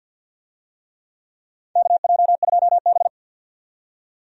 Morse code, heard as a keyed tone, about 49 wpm, 700 Hz. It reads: KJ2B
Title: USA Call Signs - 4 Characters - 50wpm